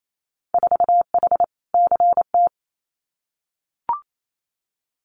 Morse code, heard as a keyed tone, about 28 wpm, 700 Hz. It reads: 4H CT
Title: Winter Field Day Exchanges Encore 28wpm